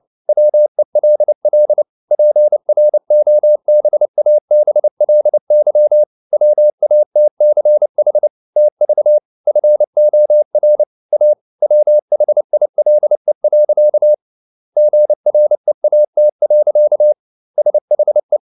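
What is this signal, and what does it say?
Morse code, heard as a keyed tone, about 29 wpm, 600 Hz. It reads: WELL PROBABLY WATCH TV FOR A WHILE. GREAT. SHE
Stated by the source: Morse code at 29 wpm, sending Ken Follett's novel Never in English, from CW med Gnister.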